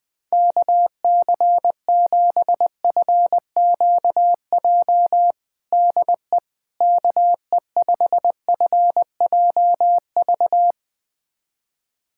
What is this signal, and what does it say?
Morse code, heard as a keyed tone, about 20 wpm, 700 Hz. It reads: KC7FQJ DE KE5FJV
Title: USA Callsign DE Callsign - 6 Characters - 20wpm